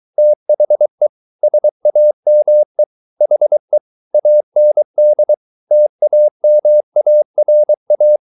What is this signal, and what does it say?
Morse code, heard as a keyed tone, about 23 wpm, 600 Hz. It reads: THE SAME HE AND TAMARA